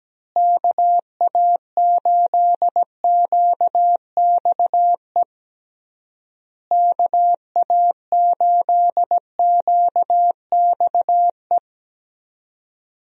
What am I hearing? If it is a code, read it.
Morse code, heard as a keyed tone, about 17 wpm, 700 Hz. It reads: KA8QXE KA8QXE
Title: USA Call Signs - 6 Characters - Repeated 3x - 17wpm